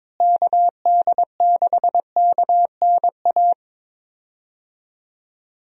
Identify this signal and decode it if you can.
Morse code, heard as a keyed tone, about 22 wpm, 700 Hz. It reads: KD6KNA
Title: USA Call Signs - 6 Characters - Repeated 2x - 22wpm